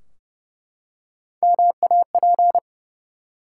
Morse code, heard as a keyed tone, about 30 wpm, 700 Hz. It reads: MAP
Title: Three Letter Words - 30wpm